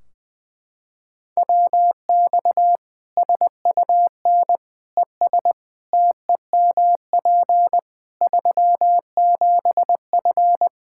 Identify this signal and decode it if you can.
Morse code, heard as a keyed tone, about 21 wpm, 700 Hz. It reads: WX SUN ES TEMP 37F
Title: K1USN SST QSO Practice 20wpm